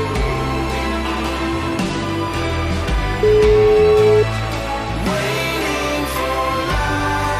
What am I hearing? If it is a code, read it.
Ringback tone (Europe, 425 Hz), with noise.